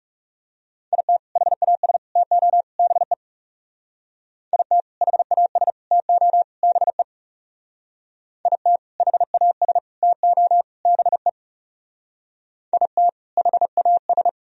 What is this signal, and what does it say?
Morse code, heard as a keyed tone, about 38 wpm, 700 Hz. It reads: IT HAS TO BE IT HAS TO BE IT HAS TO BE IT HAS